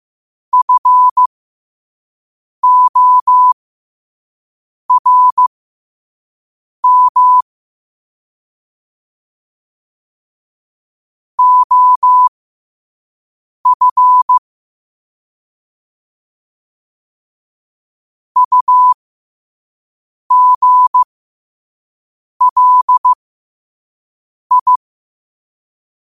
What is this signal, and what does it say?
Morse code, heard as a keyed tone, about 15 wpm, spaced as for 5 wpm, 1000 Hz. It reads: FORM OF UGLI
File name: qotdfiles / 5wpm